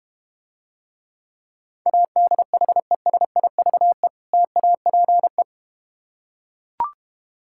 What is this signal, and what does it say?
Morse code, heard as a keyed tone, about 32 wpm, 700 Hz. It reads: ADHESIVE TAPE